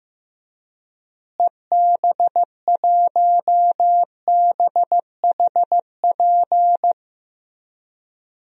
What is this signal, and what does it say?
Morse code, heard as a keyed tone, about 15 wpm, 700 Hz. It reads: EB1BHP
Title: DXCC-100 Call Signs - 6 Characters - Repeated 2x - 15wpm